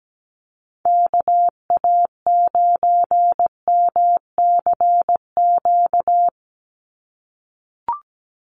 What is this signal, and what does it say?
Morse code, heard as a keyed tone, about 17 wpm, 700 Hz. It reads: KA9MCQ
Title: USA Call Signs 17wpm